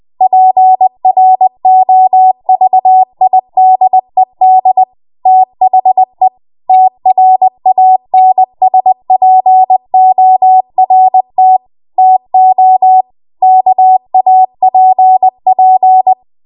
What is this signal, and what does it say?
Morse code, heard as a keyed tone, about 20 wpm, 750 Hz. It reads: PROVIDED THE TRANSPORT TO KAPP
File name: Morse / 20WPM